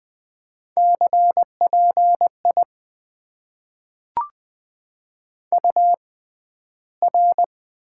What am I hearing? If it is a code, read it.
Morse code, heard as a keyed tone, about 20 wpm, spaced as for 12 wpm, 700 Hz. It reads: CPI UR
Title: Top 90 QSO Elements - 6x Character Spacing - 20wpm